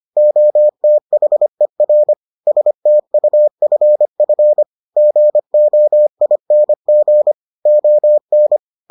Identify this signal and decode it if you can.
Morse code, heard as a keyed tone, about 25 wpm, 600 Hz. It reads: OTHER STUFF GOING ON